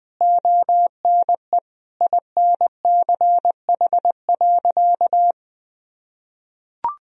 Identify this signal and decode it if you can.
Morse code, heard as a keyed tone, about 21 wpm, 700 Hz. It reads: ONE INCH.
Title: Sentences from Top 1000 Words - Review Part 3 - 20wpm